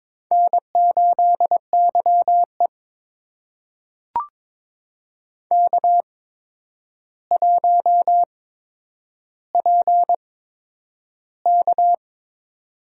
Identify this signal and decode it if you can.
Morse code, heard as a keyed tone, about 22 wpm, spaced as for 9 wpm, 700 Hz. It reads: N8YE K1PK
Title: USA Call Signs - 4 Characters - 8x Character Spacing - 22wpm